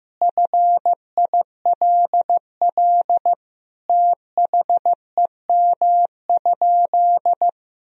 Morse code, heard as a keyed tone, about 15 wpm, 700 Hz. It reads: FILL THEM?
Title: Sentences from Top 1000 Words - Review Part 3 - 15wpm